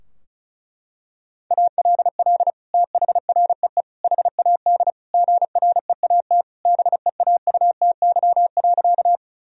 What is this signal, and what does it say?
Morse code, heard as a keyed tone, about 34 wpm, 700 Hz. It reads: ALL THREE HAD GREAT BEAUTY.
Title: Sentences from Top 1000 Words - Review Part 2 - 35wpm